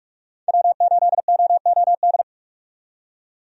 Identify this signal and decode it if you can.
Morse code, heard as a keyed tone, about 46 wpm, 700 Hz. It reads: W8OOD
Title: USA Call Signs - 5 Characters - 45wpm